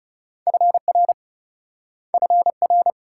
Morse code, heard as a keyed tone, about 38 wpm, 700 Hz. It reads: FR FR